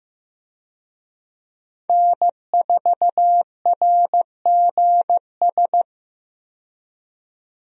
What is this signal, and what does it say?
Morse code, heard as a keyed tone, about 15 wpm, 700 Hz. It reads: N4RGS
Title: USA Call Signs - 5 Characters - 15wpm